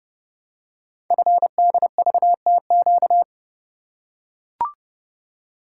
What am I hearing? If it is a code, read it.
Morse code, heard as a keyed tone, about 30 wpm, 700 Hz. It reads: FDVTQ
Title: Code Groups - 5 Letters - 30wpm